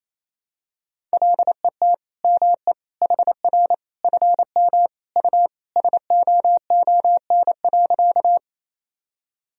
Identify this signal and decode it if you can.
Morse code, heard as a keyed tone, about 27 wpm, 700 Hz. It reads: LET ME HR FM U SOON.
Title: Sentences from Top 400 Words and Abbreviations 28wpm